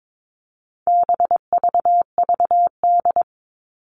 Morse code, heard as a keyed tone, about 22 wpm, 700 Hz. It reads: BVVD
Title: All Cut Numbers - 4-Digit 22wpm